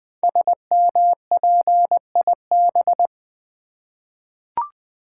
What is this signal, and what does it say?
Morse code, heard as a keyed tone, about 20 wpm, 700 Hz. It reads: SMPIB
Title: Code Groups - 5 Letters - 20wpm